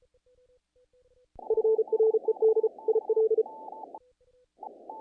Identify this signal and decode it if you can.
Morse code, heard as a keyed tone, about 32 wpm, 440 Hz. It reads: FRED IL
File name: contests / naqp-cw-k5tr-op-k5pi- -1800